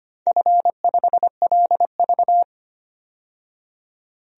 Morse code, heard as a keyed tone, about 25 wpm, 700 Hz. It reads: F5LV
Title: DXCC-100 Call Signs - 4 Characters - 25wpm